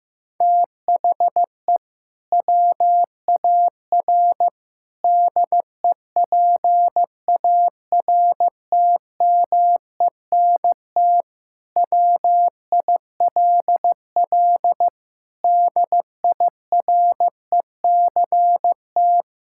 Morse code, heard as a keyed tone, about 15 wpm, 700 Hz. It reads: THE WAR DEPARTMENT WILL DIRECT
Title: Sentences from Top 1000 Words - Review Part 5 - 15wpm